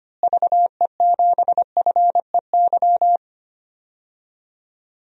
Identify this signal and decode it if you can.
Morse code, heard as a keyed tone, about 25 wpm, 700 Hz. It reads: VE7FEY
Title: DXCC-100 Call Signs - 6 Characters - Repeated 3x - 25wpm